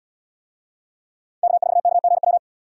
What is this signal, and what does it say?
Morse code, heard as a keyed tone, about 73 wpm, 700 Hz. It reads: K4KGW